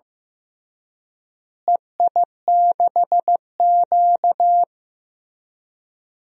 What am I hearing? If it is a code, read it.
Morse code, heard as a keyed tone, about 15 wpm, 700 Hz. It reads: EI6Q
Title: DXCC-100 Call Signs - 4 Characters - Repeated 3x - 15wpm